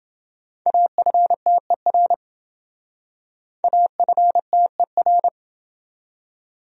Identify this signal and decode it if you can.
Morse code, heard as a keyed tone, about 27 wpm, 700 Hz. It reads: AFTER AFTER